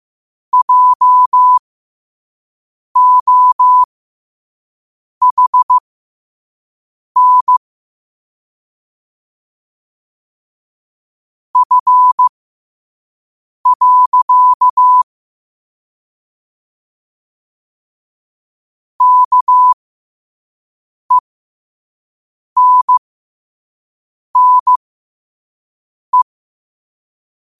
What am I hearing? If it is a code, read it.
Morse code, heard as a keyed tone, about 15 wpm, spaced as for 5 wpm, 1000 Hz. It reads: JOHN F. KENNE